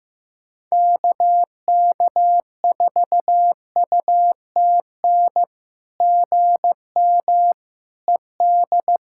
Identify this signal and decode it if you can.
Morse code, heard as a keyed tone, about 15 wpm, 700 Hz. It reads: KK4UTN GM ED
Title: SOTA QSO Practice 15wpm